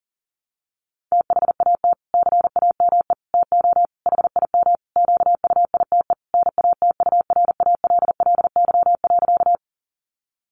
Morse code, heard as a keyed tone, about 40 wpm, 700 Hz. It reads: THAT CAME TO HIM QUITE NATURALLY.
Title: Sentences from Top 1300 Words 40wpm